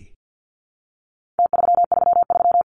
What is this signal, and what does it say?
Morse code, heard as a keyed tone, about 50 wpm, 700 Hz. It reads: T333
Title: Numbers - Contest Style - 4-Digit 50wpm